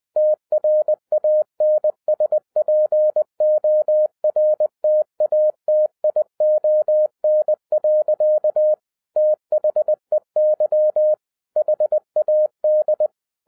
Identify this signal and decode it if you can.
Morse code, heard as a keyed tone, about 20 wpm, 610 Hz. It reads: TRANSPORTATION. THEY HAD
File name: 2011-07-mp3 / n33 THE 3.11 2 20wpm